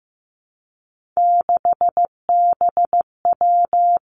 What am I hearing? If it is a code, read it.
Morse code, heard as a keyed tone, about 15 wpm, 700 Hz. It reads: 6BW
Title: Three Challenging Letter-Number Combinations 15wpm